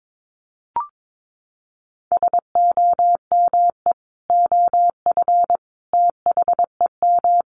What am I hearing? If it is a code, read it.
Morse code, heard as a keyed tone, about 22 wpm, 700 Hz. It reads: SOME OF THEM
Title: Sentences from Top 1200 Words 22wpm